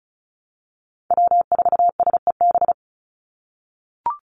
Morse code, heard as a keyed tone, about 34 wpm, 700 Hz. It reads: W4SEB
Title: USA Call Signs 35wpm